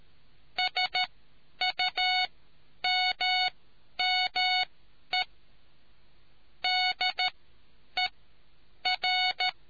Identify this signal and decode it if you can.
Morse code, heard as a keyed tone, about 13 wpm, spaced as for 9 wpm, 740 Hz. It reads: SUMME DER